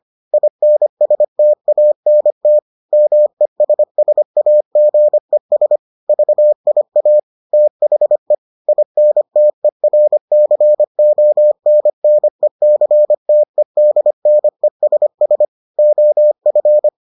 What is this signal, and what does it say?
Morse code, heard as a keyed tone, about 25 wpm, 600 Hz. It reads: INSTANT MESSAGES VIA THE INTERCONNECTEDNESS OF